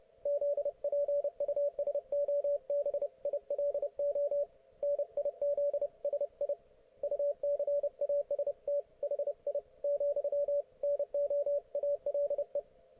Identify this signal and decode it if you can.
Morse code, heard as a keyed tone, about 30 wpm, 570 Hz. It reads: ZPUSOBILO NIZSI UCAST HI , NO ALE